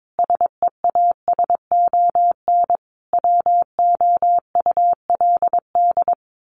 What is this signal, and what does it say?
Morse code, heard as a keyed tone, about 22 wpm, 700 Hz. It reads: SEASON WOULD
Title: Sentences from Top 1400 Words 22wpm